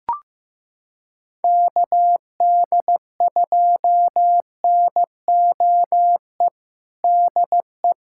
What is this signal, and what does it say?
Morse code, heard as a keyed tone, about 16 wpm, 700 Hz. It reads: KD2NOE DE
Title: USA Callsign DE Callsign - 6 Characters - 15wpm